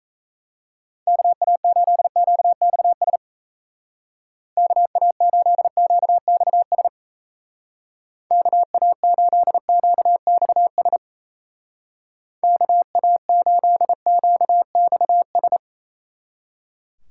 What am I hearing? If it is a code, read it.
Morse code, heard as a keyed tone, about 34 wpm, 700 Hz. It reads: KA8QXS KA8QXS KA8QXS KA8QXS
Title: Speed Racing - USA 2x3 Call Signs 28wpm